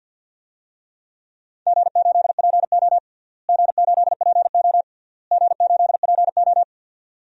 Morse code, heard as a keyed tone, about 46 wpm, 700 Hz. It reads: G8PO G8PO G8PO
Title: DXCC-100 Call Signs - 4 Characters - Repeated 3x - 50wpm